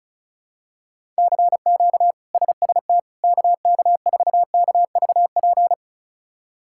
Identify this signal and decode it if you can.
Morse code, heard as a keyed tone, about 36 wpm, 700 Hz. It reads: CQ SST KK4KVP
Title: K1USN SST QSO Practice - 35wpm